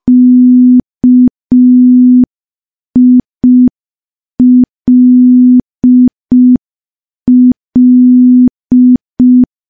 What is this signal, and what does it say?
Morse code, heard as a keyed tone, about 5 wpm, 250 Hz. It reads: KILL
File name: morse / morse-training-05WPM